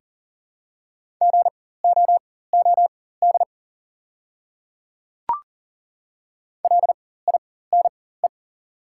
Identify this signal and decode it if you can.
Morse code, heard as a keyed tone, about 40 wpm, spaced as for 19 wpm, 690 Hz. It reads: GOOD LINE
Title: 4x Character Spacing - Top 200 Words - 40wpm